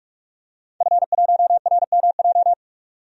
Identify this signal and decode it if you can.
Morse code, heard as a keyed tone, about 47 wpm, 700 Hz. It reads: F1RMJ
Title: DXCC-100 Call Signs - 5 Characters - 45wpm